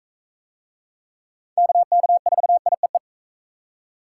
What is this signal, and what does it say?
Morse code, heard as a keyed tone, about 44 wpm, 700 Hz. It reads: KK4IEE